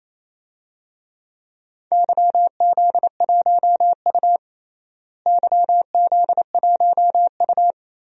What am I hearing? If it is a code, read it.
Morse code, heard as a keyed tone, about 27 wpm, 700 Hz. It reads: YZ1U YZ1U
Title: DXCC-100 Call Signs - 4 Characters - Repeated 2x - 28wpm